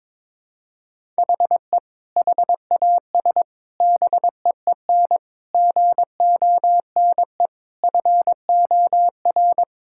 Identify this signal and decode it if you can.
Morse code, heard as a keyed tone, about 22 wpm, 700 Hz. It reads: HE HAS BEEN GONE FOR
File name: practice / 22wpm / Sentences with 4-Word Phrases - 22wpm